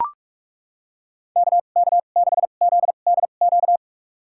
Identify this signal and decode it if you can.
Morse code, heard as a keyed tone, about 41 wpm, spaced as for 31 wpm, 700 Hz. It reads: KK6ZDQ
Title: USA Call Signs - 6 Characters - 2x Character Spacing - 45wpm